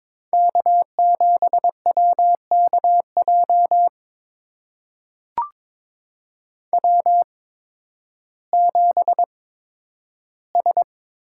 Morse code, heard as a keyed tone, about 22 wpm, spaced as for 12 wpm, 700 Hz. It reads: K7WKJ W7S
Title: USA Call Signs - 5 Characters - 8x Character Spacing - 22wpm